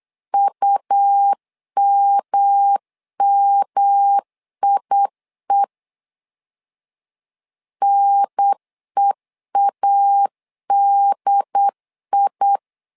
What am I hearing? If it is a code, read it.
Morse code, heard as a keyed tone, about 8 wpm, 800 Hz. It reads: UMMIE NEADI
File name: LEZIONI / 3 MWRO / 3 M 8wpm 2300